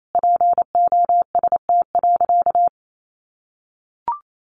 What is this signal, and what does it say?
Morse code, heard as a keyed tone, about 28 wpm, 700 Hz. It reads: POST.
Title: Sentences from Top 1000 Words 28wpm